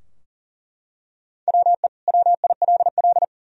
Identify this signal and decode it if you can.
Morse code, heard as a keyed tone, about 37 wpm, 700 Hz. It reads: WE WILL
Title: Sentences from Top 1000 Words - Review Part 3 - 40wpm